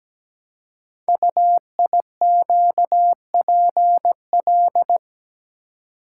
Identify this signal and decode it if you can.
Morse code, heard as a keyed tone, about 17 wpm, 700 Hz. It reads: UIQPL
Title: Code Groups - 5 Letters - 17wpm